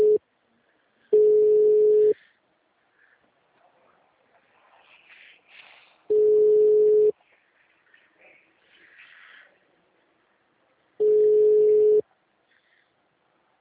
Ringback tone (Europe, 425 Hz).